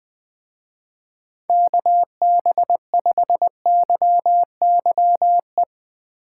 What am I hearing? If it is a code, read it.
Morse code, heard as a keyed tone, about 19 wpm, 700 Hz. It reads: KB5YYE
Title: Speed Racing - USA Call Signs 17wpm